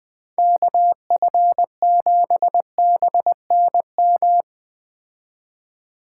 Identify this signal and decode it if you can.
Morse code, heard as a keyed tone, about 21 wpm, 700 Hz. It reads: KF7BNM